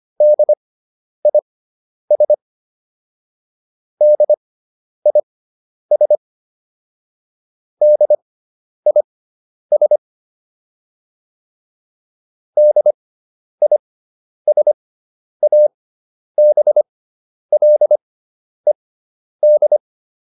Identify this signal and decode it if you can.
Morse code, heard as a keyed tone, about 25 wpm, spaced as for 10 wpm, 600 Hz. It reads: DIS DIS DIS DISABLED